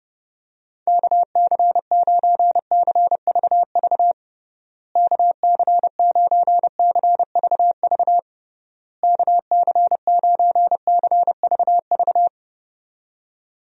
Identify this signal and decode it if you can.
Morse code, heard as a keyed tone, about 30 wpm, 700 Hz. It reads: KC9CVV KC9CVV KC9CVV
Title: USA Call Signs - 6 Characters - Repeated 3x - 30wpm